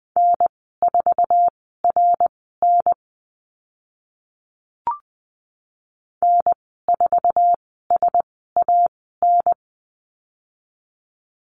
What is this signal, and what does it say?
Morse code, heard as a keyed tone, about 20 wpm, spaced as for 15 wpm, 700 Hz. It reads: N4RN N4SAN
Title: 062 Lesson 11g - Letter h - Call Signs 20wpm